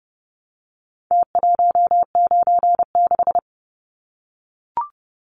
Morse code, heard as a keyed tone, about 30 wpm, 700 Hz. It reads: T196
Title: Numbers - Contest Style - 4-Digit 30wpm